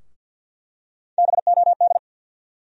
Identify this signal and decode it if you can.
Morse code, heard as a keyed tone, about 48 wpm, 700 Hz. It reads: BOD
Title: USA Call Sign Postfixes - 3 Letters 50wpm